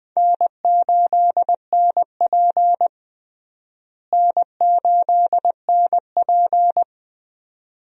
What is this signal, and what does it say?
Morse code, heard as a keyed tone, about 20 wpm, 700 Hz. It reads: N8NP N8NP
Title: USA Call Signs - 4 Characters - Repeated 2x - 20wpm